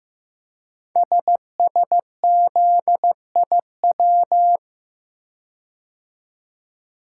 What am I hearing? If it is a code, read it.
Morse code, heard as a keyed tone, about 15 wpm, 690 Hz. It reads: SSZIW
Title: Code Groups - 5 Letters - 15wpm